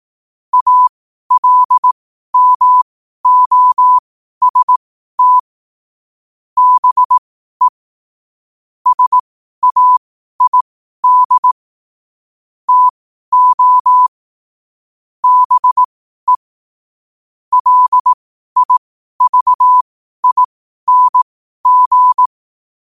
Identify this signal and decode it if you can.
Morse code, heard as a keyed tone, about 18 wpm, spaced as for 12 wpm, 1000 Hz. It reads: ALMOST BE SAID TO BE LIVING
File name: qotdfiles / 12wpm